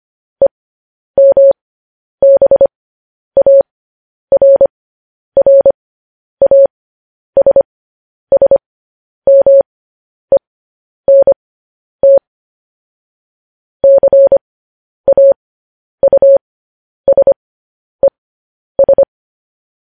Morse code, heard as a keyed tone, about 25 wpm, spaced as for 10 wpm, 550 Hz. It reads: EMBARRASSMENT CAUSES
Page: news headlines in morse code at 10 wpm